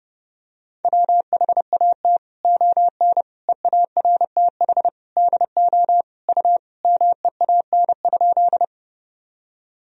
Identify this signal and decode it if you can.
Morse code, heard as a keyed tone, about 30 wpm, 700 Hz. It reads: WHAT ON EARTH DO U MEAN?
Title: Sentences from Top 200 Words and Abbreviations 30wpm